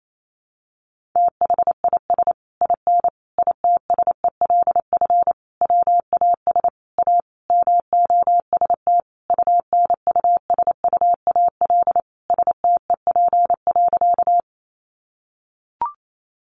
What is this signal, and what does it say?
Morse code, heard as a keyed tone, about 28 wpm, 700 Hz. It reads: THIS IN ITSELF WAS A MOST UNUSUAL STEP.
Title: Sentences from Top 1400 Words 28wpm